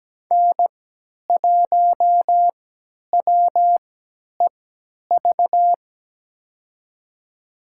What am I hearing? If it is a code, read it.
Morse code, heard as a keyed tone, about 17 wpm, spaced as for 10 wpm, 700 Hz. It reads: N1WEV
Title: USA Call Signs - 5 Characters - 3x Character Spacing - 17wpm